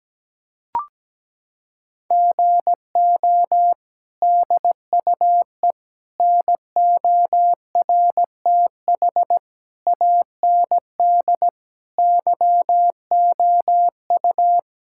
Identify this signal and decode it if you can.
Morse code, heard as a keyed tone, about 17 wpm, 700 Hz. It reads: GO DUE NORTH AND YOU
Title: Sentences from Top 1000 Words - Review Part 6 - 17wpm